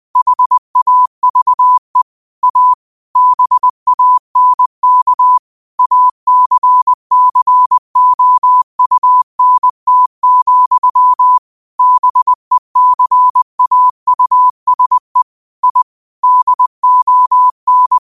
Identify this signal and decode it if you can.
Morse code, heard as a keyed tone, about 20 wpm, 1000 Hz. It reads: HAVE A BANK ACCOUNT, BECAUSE I DON
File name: qotdfiles / 20wpm